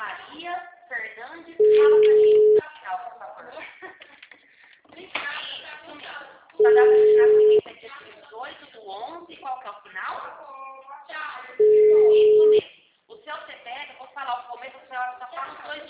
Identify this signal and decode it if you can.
Ringback tone (Europe, 425 Hz), with speech.
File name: out- -2001- - - .23346